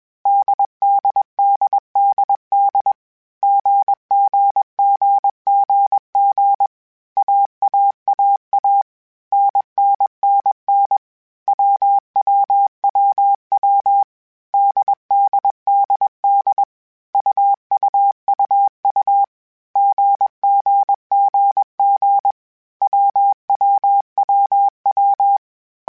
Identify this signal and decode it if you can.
Morse code, heard as a keyed tone, about 21 wpm, 800 Hz. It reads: DDDDD GGGGG AAAA NNNN WWWW DDDD UUUU GGGG WWWW
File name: Groups@20 / Groups@20 mp3s / AWUNDG@20